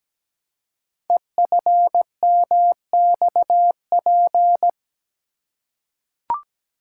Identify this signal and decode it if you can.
Morse code, heard as a keyed tone, about 17 wpm, 700 Hz. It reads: EFMXP
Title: Code Groups - 5 Letters - 17wpm